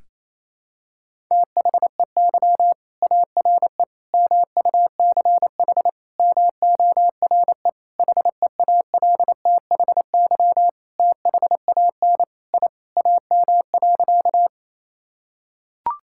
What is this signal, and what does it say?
Morse code, heard as a keyed tone, about 28 wpm, 700 Hz. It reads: THEY ARE MUCH MORE HEALTHY THAN I AM.
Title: Sentences from Top 1000 Words - Review Part 5 - 28wpm